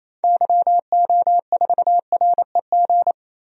Morse code, heard as a keyed tone, about 28 wpm, 700 Hz. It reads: YO4REG
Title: DXCC-100 Call Signs - 6 Characters - Repeated 3x - 28wpm